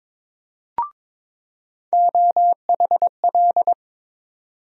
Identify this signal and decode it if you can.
Morse code, heard as a keyed tone, about 22 wpm, 700 Hz. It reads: OHL
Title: USA Call Sign Postfixes - 3 Letters 22wpm